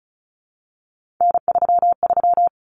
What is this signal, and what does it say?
Morse code, heard as a keyed tone, about 34 wpm, 700 Hz. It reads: N33
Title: Common Cut Numbers - 3-Digit 35wpm